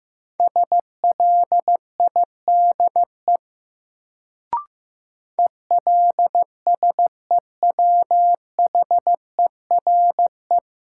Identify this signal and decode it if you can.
Morse code, heard as a keyed tone, about 15 wpm, 700 Hz. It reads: SLIDE ELSEWHERE
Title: Sentences from Top 1600 Words - New Word Drill - 15wpm